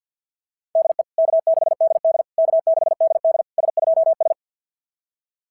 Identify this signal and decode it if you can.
Morse code, heard as a keyed tone, about 46 wpm, 690 Hz. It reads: DE K6DD K6DD S2S
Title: SOTA QSO Practice 50wpm